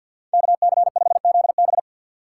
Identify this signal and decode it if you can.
Morse code, heard as a keyed tone, about 53 wpm, 700 Hz. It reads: KC5ZB
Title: USA Call Signs - 5 Characters - 4x Character Spacing - 50wpm